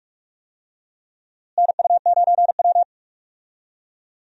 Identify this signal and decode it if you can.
Morse code, heard as a keyed tone, about 46 wpm, 700 Hz. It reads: NU9W
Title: USA Call Signs - 4 Characters - Repeated 3x - 45wpm